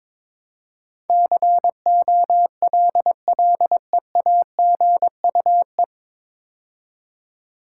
Morse code, heard as a keyed tone, about 22 wpm, 700 Hz. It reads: COLLEAGUE
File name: 9 Letter Words - First 3 Letters Given - 22wpm